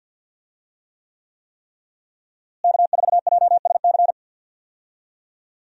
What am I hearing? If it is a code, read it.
Morse code, heard as a keyed tone, about 46 wpm, 700 Hz. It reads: K4JSC
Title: USA Call Signs - 5 Characters - Repeated 3x - 50wpm